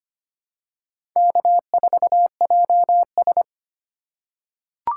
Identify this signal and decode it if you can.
Morse code, heard as a keyed tone, about 25 wpm, 700 Hz. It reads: K4JS
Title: USA Call Signs - 4 Characters - 2x Character Spacing - 25wpm